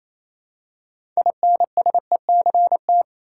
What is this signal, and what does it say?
Morse code, heard as a keyed tone, about 28 wpm, 700 Hz. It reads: INSECT
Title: Six Letter Words - 28wpm